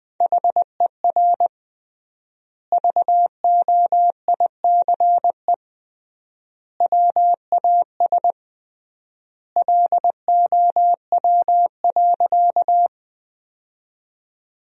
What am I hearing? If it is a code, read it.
Morse code, heard as a keyed tone, about 21 wpm, 700 Hz. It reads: HER VOICE WAS LOW.
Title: 3x Word-Spacing - Sentences from Top 500 Words - 20wpm